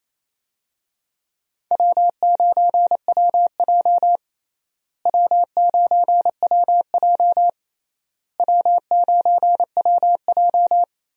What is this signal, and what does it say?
Morse code, heard as a keyed tone, about 28 wpm, 700 Hz. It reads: W9WJ W9WJ W9WJ
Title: USA Call Signs - 4 Characters - Repeated 3x - 28wpm